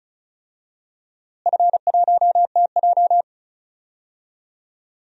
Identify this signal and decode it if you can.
Morse code, heard as a keyed tone, about 34 wpm, 700 Hz. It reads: F1TJ
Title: DXCC-100 Call Signs - 4 Characters - Repeated 3x - 35wpm